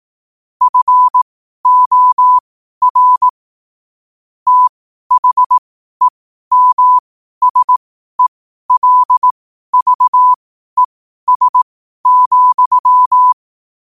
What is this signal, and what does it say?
Morse code, heard as a keyed tone, about 18 wpm, spaced as for 12 wpm, 1000 Hz. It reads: FOR THEMSELVES,
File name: qotdfiles / 12wpm